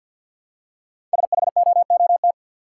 Morse code, heard as a keyed tone, about 50 wpm, 700 Hz. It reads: SHOOT
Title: Five Letter Words - 50wpm